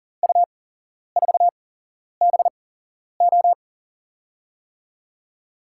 Morse code, heard as a keyed tone, about 40 wpm, spaced as for 11 wpm, 700 Hz. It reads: U4BO
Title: USA Call Signs - 5 Characters - 8x Character Spacing - 40wpm